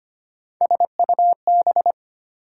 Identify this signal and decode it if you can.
Morse code, heard as a keyed tone, about 25 wpm, 700 Hz. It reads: SUB
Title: Prefixes and Suffixes 25wpm